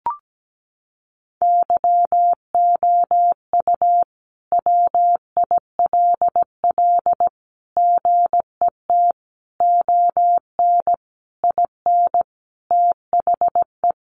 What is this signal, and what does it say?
Morse code, heard as a keyed tone, about 17 wpm, 700 Hz. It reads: YOU WILL GET ON IN THE
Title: Sentences from Top 200 Words 17wpm